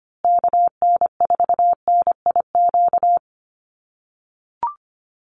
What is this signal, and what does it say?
Morse code, heard as a keyed tone, about 25 wpm, 700 Hz. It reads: KN4NIQ
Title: USA Call Signs Repeated 3x 25wpm